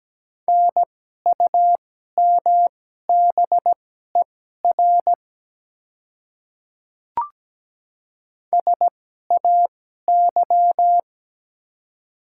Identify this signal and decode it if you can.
Morse code, heard as a keyed tone, about 17 wpm, spaced as for 12 wpm, 700 Hz. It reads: NUMBER SAY
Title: 2x Character Spacing - Top 200 Words - 17wpm